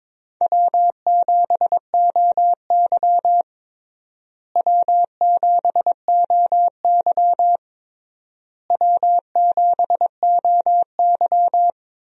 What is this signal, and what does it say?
Morse code, heard as a keyed tone, about 22 wpm, 700 Hz. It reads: W7OY W7OY W7OY
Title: USA Call Signs - 4 Characters - Repeated 3x - 22wpm